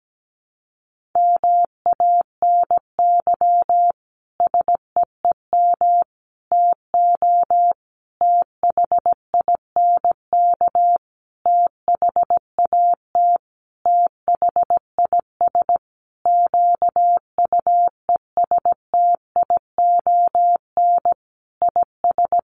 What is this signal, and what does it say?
Morse code, heard as a keyed tone, about 17 wpm, 700 Hz. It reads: MANY SEEM TO THINK THAT THIS QUESTION IS
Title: Sentences from Top 1400 Words 17wpm